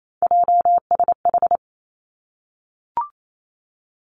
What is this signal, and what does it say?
Morse code, heard as a keyed tone, about 28 wpm, 690 Hz. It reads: JSH
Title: Three Challenging Letter-Number Combinations 28wpm